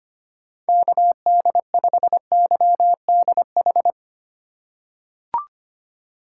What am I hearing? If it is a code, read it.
Morse code, heard as a keyed tone, about 25 wpm, 700 Hz. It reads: KD5YDH
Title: USA Call Signs - 6 Characters - 4x Character Spacing - 25wpm